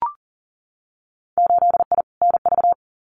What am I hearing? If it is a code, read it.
Morse code, heard as a keyed tone, about 40 wpm, 700 Hz. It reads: 8I NV
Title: Winter Field Day Exchanges Encore 40wpm